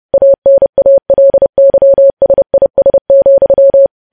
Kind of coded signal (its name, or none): Morse code